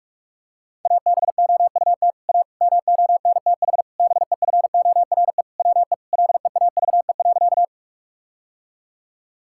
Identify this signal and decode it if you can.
Morse code, heard as a keyed tone, about 44 wpm, 690 Hz. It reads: ABOUT A MONTH BEFORE WE LEAVE.
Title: Sentences from Top 1000 Words - Review Part 5 - 45wpm